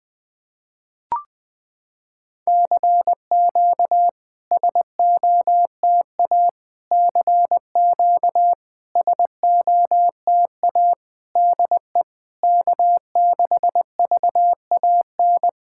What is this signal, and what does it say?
Morse code, heard as a keyed tone, about 20 wpm, 700 Hz. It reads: CQ SOTA CQ SOTA DE K6VAN
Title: SOTA QSO Practice 20wpm